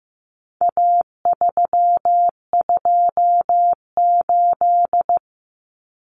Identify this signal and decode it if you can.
Morse code, heard as a keyed tone, about 15 wpm, 700 Hz. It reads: A328
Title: Common Cut Numbers - 4-Digit 15wpm